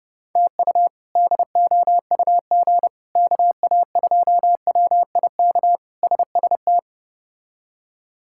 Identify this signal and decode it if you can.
Morse code, heard as a keyed tone, about 30 wpm, 700 Hz. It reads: TU DOUG KA2WIK SST
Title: K1USN SST QSO Practice - 30wpm